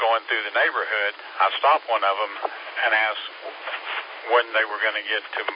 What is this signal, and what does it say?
Emergency Alert System (SAME) tones, heard as harsh buzzing bursts, with speech. It is heard from 0.8 to 5.0 s.